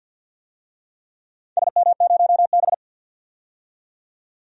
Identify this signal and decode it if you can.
Morse code, heard as a keyed tone, about 47 wpm, 700 Hz. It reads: SM0B
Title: DXCC-100 Call Signs - 4 Characters - 50wpm